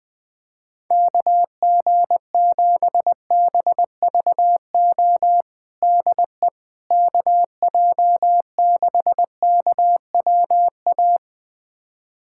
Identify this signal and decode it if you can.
Morse code, heard as a keyed tone, about 20 wpm, 700 Hz. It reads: KG7BVO DE KJ6KWA
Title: USA Callsign DE Callsign - 6 Characters - 20wpm